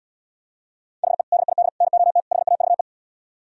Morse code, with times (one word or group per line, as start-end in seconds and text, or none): 1.0-1.2 HE
1.3-1.7 DID
1.8-2.2 NOT
2.3-2.8 HAVE